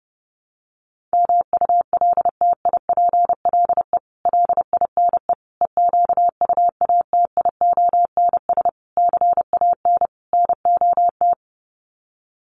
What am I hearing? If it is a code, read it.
Morse code, heard as a keyed tone, about 30 wpm, 700 Hz. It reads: MULTIPLE LINE EQUATIONS CAN NOT
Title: Sets of 5 Words 30wpm